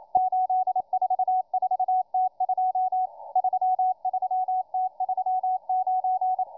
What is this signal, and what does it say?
Morse code, heard as a keyed tone, about 27 wpm, 740 Hz, with radio noise. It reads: 844T2 33T39